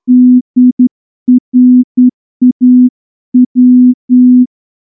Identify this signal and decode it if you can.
Morse code, heard as a keyed tone, about 10 wpm, 250 Hz. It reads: DRAW
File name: morse-training-10WPM